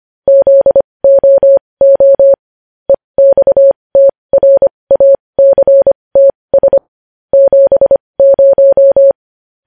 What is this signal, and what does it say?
Morse code, heard as a keyed tone, about 25 wpm, spaced as for 20 wpm, 550 Hz. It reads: ZOO EXTRACTS 70